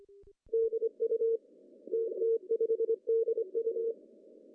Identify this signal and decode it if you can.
Morse code, heard as a keyed tone, about 26 wpm, 440 Hz. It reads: DU TT5DU
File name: CW / cqww-cw-k5tr